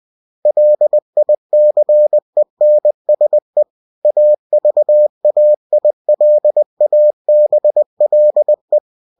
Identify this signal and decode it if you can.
Morse code, heard as a keyed tone, about 20 wpm, 600 Hz. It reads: LICENSE AVAILABLE
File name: cw-books / winnie-the-pooh / winnie-20wpm-page-0113